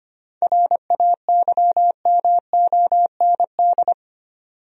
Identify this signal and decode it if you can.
Morse code, heard as a keyed tone, about 25 wpm, 700 Hz. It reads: RAYMOND
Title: USA Common Names - 25wpm